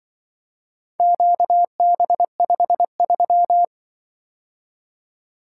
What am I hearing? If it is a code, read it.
Morse code, heard as a keyed tone, about 24 wpm, 700 Hz. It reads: QB53
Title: Speed-Racing - Four Letter-Number Combinations - 20wpm